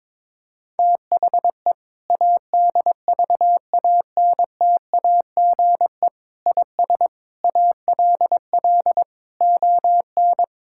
Morse code, heard as a keyed tone, about 22 wpm, 700 Hz. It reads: THE ADVANTAGE IS ALL ON
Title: Sentences from Top 1000 Words - Review Part 6 - 22wpm